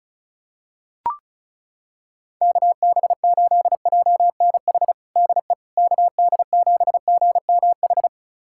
Morse code, heard as a keyed tone, about 36 wpm, 700 Hz. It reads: KB8JNH DE KD7GMH
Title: USA Callsign DE Callsign - 6 Characters - 35wpm